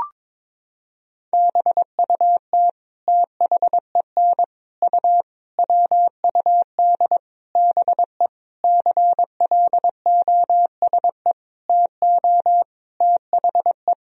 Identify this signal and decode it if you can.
Morse code, heard as a keyed tone, about 21 wpm, 700 Hz. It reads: BUT THEN U WUD BE CLOSE TO THE
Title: Sentences from Top 300 Words and Abbreviations 22wpm